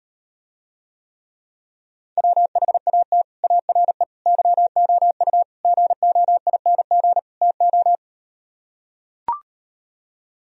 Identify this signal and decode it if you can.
Morse code, heard as a keyed tone, about 38 wpm, 700 Hz. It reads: WHAT ARE YOU GOING TO